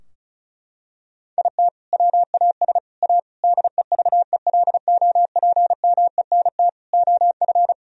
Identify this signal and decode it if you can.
Morse code, heard as a keyed tone, about 34 wpm, 700 Hz. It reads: IT WAS A DEVELOPMENT OF
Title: Sentences from Top 1000 Words - Review Part 4 - 35wpm